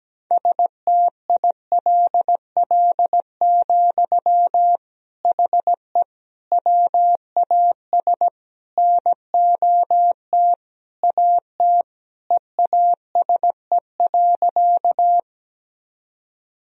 Morse code, heard as a keyed tone, about 17 wpm, 700 Hz. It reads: STILL, HE WAS NOT AT EASE.
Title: Sentences from Top 1000 Words - Review Part 1 - 17wpm